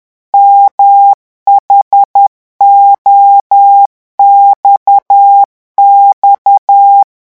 Morse code, heard as a keyed tone, about 10 wpm, 800 Hz. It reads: MHOXX
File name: morse mp3 / revision11 lettres vitesse 10